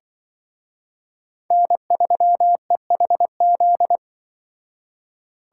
Morse code, heard as a keyed tone, about 23 wpm, 700 Hz. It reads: N3EHZ
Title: Speed Racing - USA 1x3 Call Signs 20wpm